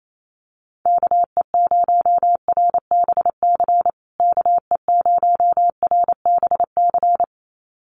Morse code, heard as a keyed tone, about 28 wpm, 700 Hz. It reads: KE0RBC KE0RBC
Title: USA Call Signs Repeated 2x 28wpm